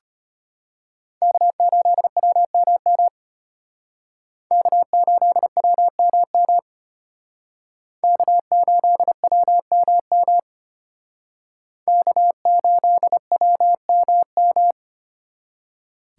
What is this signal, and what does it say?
Morse code, heard as a keyed tone, about 32 wpm, 700 Hz. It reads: K8WMM K8WMM K8WMM K8WMM